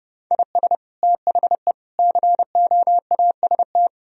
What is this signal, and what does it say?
Morse code, heard as a keyed tone, about 30 wpm, 700 Hz. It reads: IS THE COAST
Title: Sentences from Top 1000 Words - Review Part 4 - 30wpm